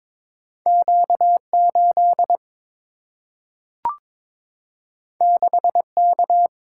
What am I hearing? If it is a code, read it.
Morse code, heard as a keyed tone, about 22 wpm, 700 Hz. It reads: Q 8 6 K
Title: Single-Letter Single-Number 22wpm